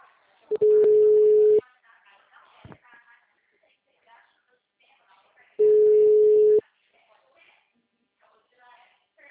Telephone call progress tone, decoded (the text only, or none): ringback tone